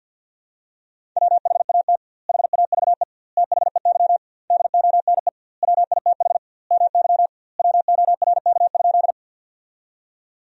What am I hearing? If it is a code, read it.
Morse code, heard as a keyed tone, about 50 wpm, 700 Hz. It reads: WHAT HAVE THEY DONE WITH MY WORK?
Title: 2x Word-Spacing - Sentences from Top 500 Words - 50wpm